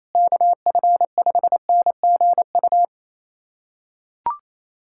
Morse code, heard as a keyed tone, about 27 wpm, 700 Hz. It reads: KF5NGU
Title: USA Callsign DE Callsign - 6 Characters - 28wpm